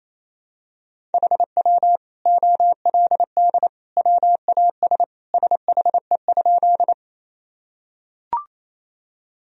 Morse code, heard as a keyed tone, about 28 wpm, 700 Hz. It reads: HW OLD WAS SHE?
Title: Sentences from Top 200 Words and Abbreviations 28wpm